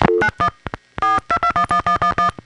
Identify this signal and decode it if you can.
DTMF, heard as telephone touch tones, tones of about 90 ms. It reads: 91 42211111